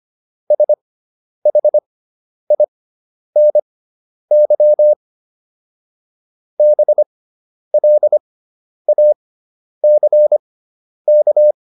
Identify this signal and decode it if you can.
Morse code, heard as a keyed tone, about 25 wpm, spaced as for 10 wpm, 600 Hz. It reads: SHINY BLACK